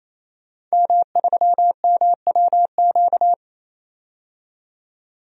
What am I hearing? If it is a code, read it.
Morse code, heard as a keyed tone, about 28 wpm, 700 Hz. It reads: M3MWQ
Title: DXCC-100 Call Signs - 5 Characters - 28wpm